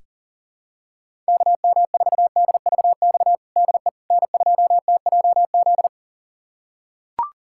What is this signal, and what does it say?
Morse code, heard as a keyed tone, about 40 wpm, 700 Hz. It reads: KM4DVX DE N2TJZ
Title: USA Callsign DE Callsign 40wpm